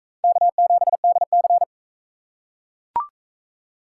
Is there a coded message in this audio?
Yes: Morse code from 0.2 to 1.6 s, heard as a keyed tone, about 39 wpm, 700 Hz. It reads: K7DC